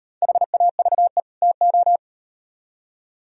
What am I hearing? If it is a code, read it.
Morse code, heard as a keyed tone, about 35 wpm, 700 Hz. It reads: HAVE TO